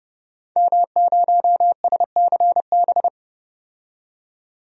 Morse code, heard as a keyed tone, about 30 wpm, 700 Hz. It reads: M0SCB
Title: DXCC-100 Call Signs - 5 Characters - 30wpm